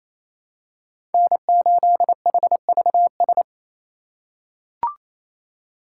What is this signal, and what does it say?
Morse code, heard as a keyed tone, about 28 wpm, 700 Hz. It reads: N8HVS
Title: USA Call Signs - 5 Characters - 4x Character Spacing - 28wpm